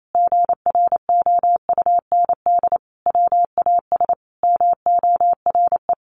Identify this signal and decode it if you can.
Morse code, heard as a keyed tone, about 28 wpm, 700 Hz. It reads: GROUND WAS MORE
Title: Sentences from Top 700 Words 28wpm